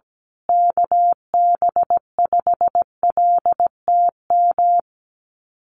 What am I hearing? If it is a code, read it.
Morse code, heard as a keyed tone, about 17 wpm, 700 Hz. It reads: KB5LTM
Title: USA Call Signs Repeated 2x 17wpm